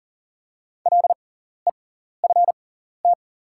Morse code, heard as a keyed tone, about 38 wpm, spaced as for 14 wpm, 690 Hz. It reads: LEFT
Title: 6x Character Spacing - Top 300 Words - 40wpm